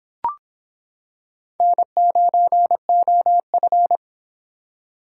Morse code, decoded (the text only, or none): N9OF